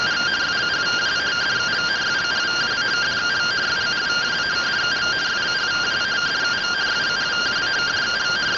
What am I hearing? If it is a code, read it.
Two-tone teleprinter signal, about 170 Hz shift.
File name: findings / AXIAL fr45 3mm.gen-17 rtty rx